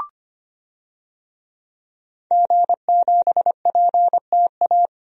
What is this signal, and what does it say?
Morse code, heard as a keyed tone, about 25 wpm, 700 Hz. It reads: G7PTA
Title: DXCC-100 Call Signs - 5 Characters - 25wpm